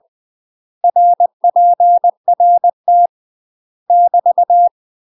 Morse code, heard as a keyed tone, about 20 wpm, 700 Hz. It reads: RPRT <BT>